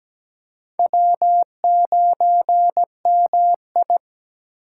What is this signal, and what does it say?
Morse code, heard as a keyed tone, about 17 wpm, 700 Hz. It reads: W9MI
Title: USA Call Signs - 4 Characters - 2x Character Spacing - 17wpm